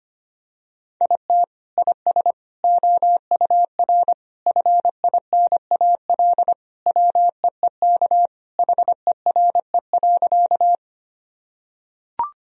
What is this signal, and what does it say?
Morse code, heard as a keyed tone, about 25 wpm, 700 Hz. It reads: IT IS OUR FINAL WEEK HERE.
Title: Sentences from Top 1000 Words - Review Part 2 - 25wpm